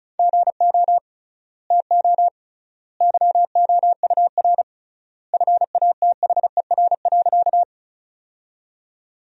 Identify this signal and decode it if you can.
Morse code, heard as a keyed tone, about 34 wpm, 700 Hz. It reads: GO TO YOUR FATHER.
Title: 3x Word-Spacing - Sentences from Top 300 Words - 35wpm